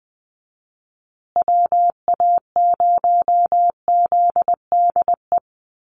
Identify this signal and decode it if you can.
Morse code, heard as a keyed tone, about 19 wpm, 700 Hz. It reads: WA0ZDE
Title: USA Call Signs Repeated 3x 20wpm